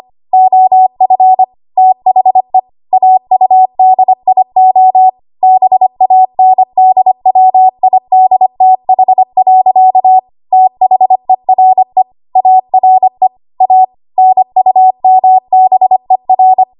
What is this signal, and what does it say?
Morse code, heard as a keyed tone, about 25 wpm, 750 Hz. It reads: OF THE AUDIO BANDWIDTH. THERE ARE A NUMBER
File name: Morse / 25WPM